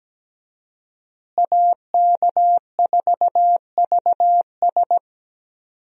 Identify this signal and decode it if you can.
Morse code, heard as a keyed tone, about 17 wpm, 700 Hz. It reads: AK4VS
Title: USA Call Signs - 5 Characters - Repeated 3x - 17wpm